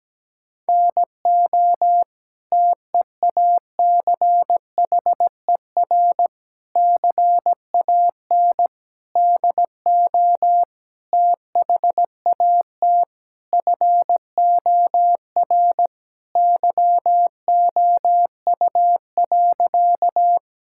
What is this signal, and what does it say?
Morse code, heard as a keyed tone, about 17 wpm, 700 Hz. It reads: NO TEACHER CAN DO THAT FOR YOU.
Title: Sentences from Top 1000 Words - Review Part 6 - 17wpm